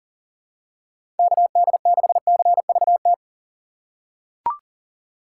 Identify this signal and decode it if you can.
Morse code, heard as a keyed tone, about 40 wpm, 700 Hz. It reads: KD6CVT
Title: USA Call Signs - 6 Characters - 3x Character Spacing - 40wpm